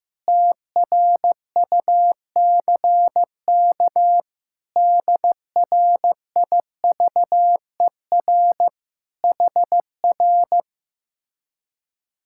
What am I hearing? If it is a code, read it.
Morse code, heard as a keyed tone, about 16 wpm, 700 Hz. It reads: TRUCK DRIVER HR
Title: Speed-Racing Common QSO Phrases Encore - 15wpm